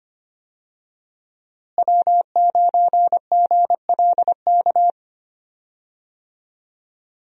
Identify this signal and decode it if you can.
Morse code, heard as a keyed tone, about 25 wpm, 700 Hz. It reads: W9GLK
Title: USA Call Signs - 5 Characters - Repeated 2x - 25wpm